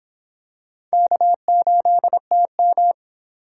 Morse code, heard as a keyed tone, about 24 wpm, 700 Hz. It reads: K8TM